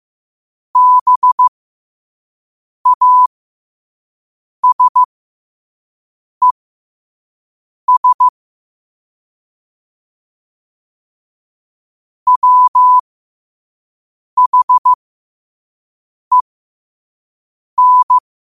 Morse code, heard as a keyed tone, about 15 wpm, spaced as for 5 wpm, 1000 Hz. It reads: BASES WHEN
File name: qotdfiles / 5wpm